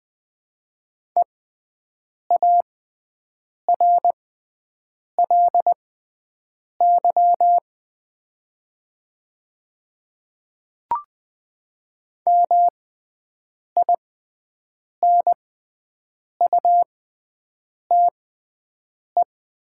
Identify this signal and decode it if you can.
Morse code, heard as a keyed tone, about 20 wpm, spaced as for 7 wpm, 700 Hz. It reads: EARLY MINUTE
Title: 6x Character Spacing - Top 500 Words - 20wpm